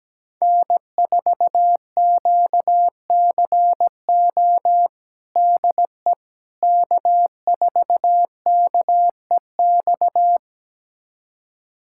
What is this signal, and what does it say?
Morse code, heard as a keyed tone, about 17 wpm, 700 Hz. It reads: N4QCO DE K4KEX
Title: USA Callsign DE Callsign 17wpm